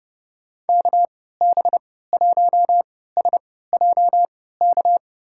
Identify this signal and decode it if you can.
Morse code, heard as a keyed tone, about 30 wpm, spaced as for 17 wpm, 700 Hz. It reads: KB1SJK
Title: USA Call Signs - 6 Characters - 3x Character Spacing - 30wpm